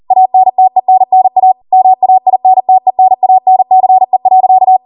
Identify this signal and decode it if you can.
Morse code, heard as a keyed tone, about 40 wpm, 750 Hz. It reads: ANTENNA MAINTENANCE.